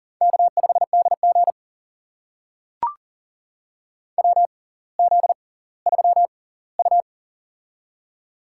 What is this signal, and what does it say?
Morse code, heard as a keyed tone, about 38 wpm, spaced as for 20 wpm, 700 Hz. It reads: K5DG WZ3U
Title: USA Call Signs - 4 Characters - 6x Character Spacing - 40wpm